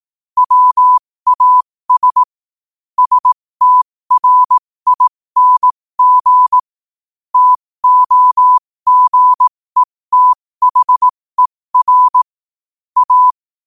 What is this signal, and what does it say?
Morse code, heard as a keyed tone, about 18 wpm, spaced as for 15 wpm, 1000 Hz. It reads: WAS STRING TOGETHER A
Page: New quotes every day in morse code at 15 Words per minute.